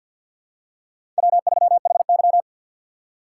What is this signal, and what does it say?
Morse code, heard as a keyed tone, about 48 wpm, 700 Hz. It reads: W3HY
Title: USA Call Signs - 6x Character Spacing - 50wpm